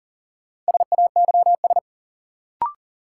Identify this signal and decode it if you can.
Morse code, heard as a keyed tone, about 40 wpm, 700 Hz. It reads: SAYS
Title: Sets of 2 Words - New Word Drill - 40wpm